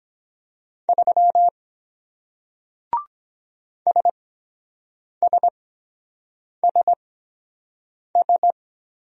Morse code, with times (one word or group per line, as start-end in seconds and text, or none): 0.9-1.5 3
3.9-8.5 SSSS